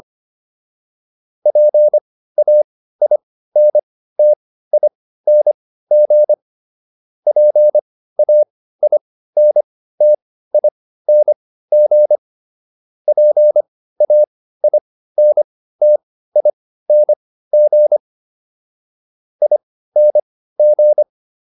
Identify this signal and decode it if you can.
Morse code, heard as a keyed tone, about 25 wpm, spaced as for 15 wpm, 600 Hz. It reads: PAINTING PAINTING PAINTING ING